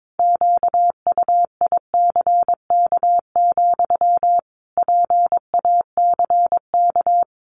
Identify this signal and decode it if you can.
Morse code, heard as a keyed tone, about 22 wpm, 700 Hz. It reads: QUICK, PACK
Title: Sentences from Top 1300 Words 22wpm